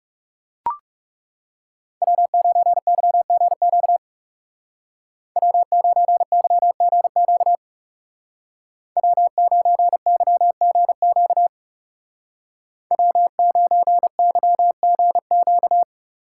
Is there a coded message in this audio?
Yes: Morse code from 2.0 to 15.8 s, heard as a keyed tone, about 39 wpm, 700 Hz. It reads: W9YGQ W9YGQ W9YGQ W9YGQ